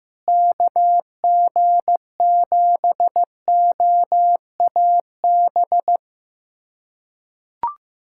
Morse code, heard as a keyed tone, about 15 wpm, 700 Hz. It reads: KG7OAB
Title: USA Callsign DE Callsign - 6 Characters - 15wpm